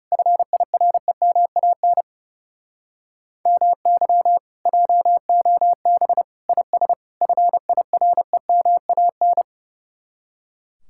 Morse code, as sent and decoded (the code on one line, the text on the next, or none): ..-. .. .-. -- .- -. / -- -.-- / .--- --- -... / .. ... / ..-. .. .-. -- .- -.
FIREMAN MY JOB IS FIREMAN